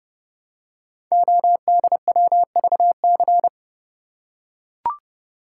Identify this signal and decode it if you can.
Morse code, heard as a keyed tone, about 28 wpm, 700 Hz. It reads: ODWVC
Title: Code Groups - 5 Letters - 30wpm